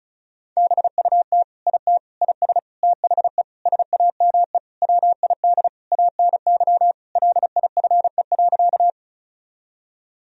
Morse code, heard as a keyed tone, about 36 wpm, 700 Hz. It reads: BUT IT IS THE SAME WID ANY LIFE.
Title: Sentences from Top 300 Words and Abbreviations 35wpm